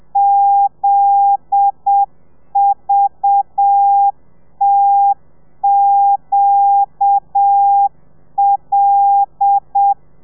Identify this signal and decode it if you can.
Morse code, heard as a keyed tone, about 7 wpm, 800 Hz. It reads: ZVTQL